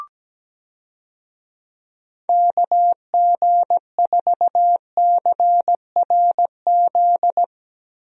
Morse code, heard as a keyed tone, about 17 wpm, 700 Hz. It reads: KG4CRZ
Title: USA Call Signs - 6 Characters - Repeated 2x - 17wpm